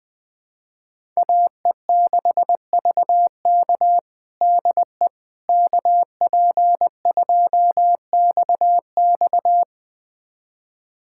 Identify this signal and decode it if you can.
Morse code, heard as a keyed tone, about 20 wpm, 700 Hz. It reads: AE6VK DE KP2XX
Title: USA Callsign DE Callsign - 5 Characters - 20wpm